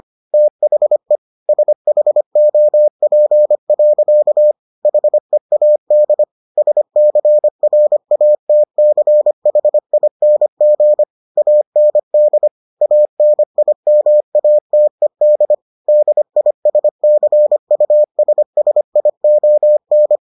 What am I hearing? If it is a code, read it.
Morse code, heard as a keyed tone, about 25 wpm, 600 Hz. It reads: THE SHOP. HEAD SCRATCHING AND ANIMATED DISCUSSION